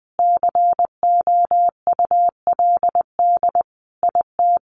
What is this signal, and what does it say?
Morse code, heard as a keyed tone, about 20 wpm, 700 Hz. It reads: COULD IT
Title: Sentences from Top 100 Words 20wpm